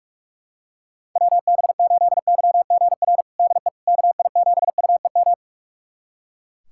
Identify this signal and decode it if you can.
Morse code, heard as a keyed tone, about 42 wpm, 690 Hz. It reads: WB8YGR DE KI7UEM
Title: USA Callsign DE Callsign - 6 Characters - 45wpm